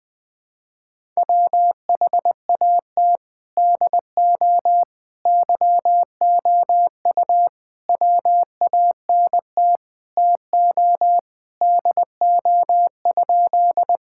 Morse code, heard as a keyed tone, about 21 wpm, 700 Hz. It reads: WHAT DO YOU WANT TO DO?
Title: Sentences with 3-Word Phrases - 20wpm